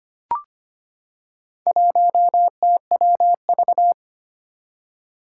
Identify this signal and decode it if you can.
Morse code, heard as a keyed tone, about 25 wpm, 700 Hz. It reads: 1TWV
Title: Four Challenging Letter-Number Combinations 25wpm